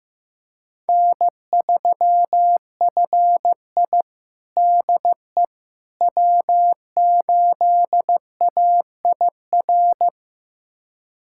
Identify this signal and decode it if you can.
Morse code, heard as a keyed tone, about 15 wpm, 700 Hz. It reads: N3FI DE W8AIR
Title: USA Callsign DE Callsign 15wpm